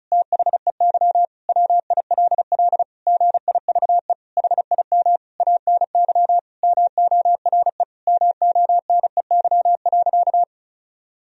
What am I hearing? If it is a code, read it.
Morse code, heard as a keyed tone, about 34 wpm, 700 Hz. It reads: THEY WILL GIVE HIM ANY MORE MONEY.
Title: Sentences from Top 1000 Words - Review Part 4 - 35wpm